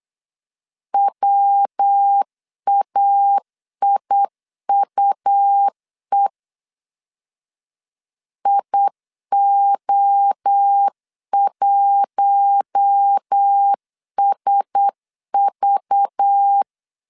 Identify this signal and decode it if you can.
Morse code, heard as a keyed tone, about 8 wpm, 800 Hz. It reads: WAIUE IO1SV
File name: CW / DET / 1 6 1500C S C 8wpm 5219